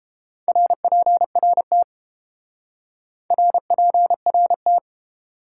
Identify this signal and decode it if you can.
Morse code, heard as a keyed tone, about 30 wpm, 700 Hz. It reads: RPRT RPRT